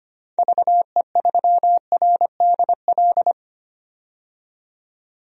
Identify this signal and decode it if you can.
Morse code, heard as a keyed tone, about 25 wpm, 700 Hz. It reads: VE3RDL
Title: DXCC-100 Call Signs - 6 Characters - 25wpm